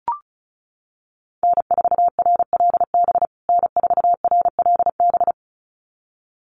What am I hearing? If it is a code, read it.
Morse code, heard as a keyed tone, about 34 wpm, 700 Hz. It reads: N4RLB N4RLB
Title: USA Call Signs Repeated 2x 35wpm